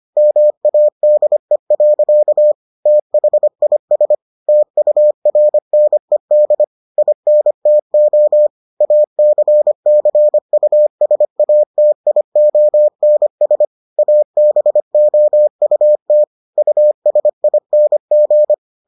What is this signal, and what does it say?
Morse code, heard as a keyed tone, about 25 wpm, 600 Hz. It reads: MADE. THIS TURNED INTO ACCUSATIONS ABOUT USING